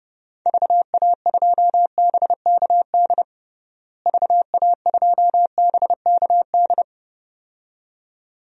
Morse code, heard as a keyed tone, about 30 wpm, 700 Hz. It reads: VA2BKD VA2BKD
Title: DXCC-100 Call Signs - 6 Characters - Repeated 2x - 30wpm